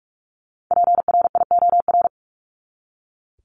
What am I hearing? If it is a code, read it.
Morse code, heard as a keyed tone, about 44 wpm, 700 Hz. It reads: PRIOR
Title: Top 1501-1750 Words 45wpm